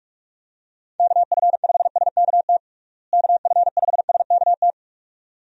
Morse code, heard as a keyed tone, about 46 wpm, 700 Hz. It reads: KF5SKT KF5SKT
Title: USA Call Signs - 6 Characters - Repeated 2x - 45wpm